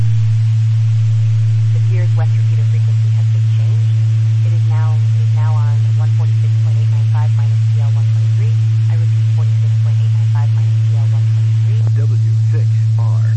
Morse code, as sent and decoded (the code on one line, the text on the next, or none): none